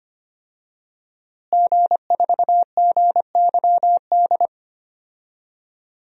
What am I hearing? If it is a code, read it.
Morse code, heard as a keyed tone, about 25 wpm, 700 Hz. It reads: G4GYD
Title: DXCC-100 Call Signs - 5 Characters - Repeated 2x - 25wpm